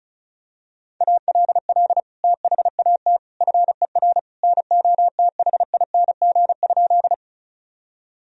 Morse code, heard as a keyed tone, about 34 wpm, 700 Hz. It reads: ALL THAT FER NOTHING?
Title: Sentences from Top 500 Words and Abbreviations 35wpm